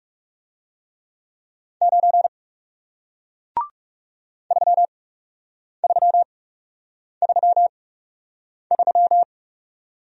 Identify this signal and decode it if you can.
Morse code, heard as a keyed tone, about 40 wpm, spaced as for 8 wpm, 700 Hz. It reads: 9 3333